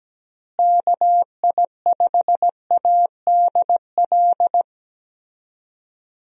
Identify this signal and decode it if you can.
Morse code, heard as a keyed tone, about 17 wpm, 700 Hz. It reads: KI5ADL
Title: USA Call Signs - 6 Characters - Repeated 3x - 17wpm